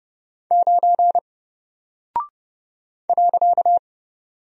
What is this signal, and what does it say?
Morse code, heard as a keyed tone, about 30 wpm, 700 Hz. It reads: 9 .
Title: Single Character (Letter:Number:Punctuation:Prosign) Alt 30wpm